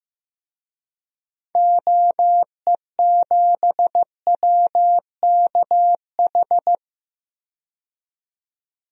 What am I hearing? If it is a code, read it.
Morse code, heard as a keyed tone, about 15 wpm, 700 Hz. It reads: OE7WKH
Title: DXCC-100 Call Signs - 6 Characters - 15wpm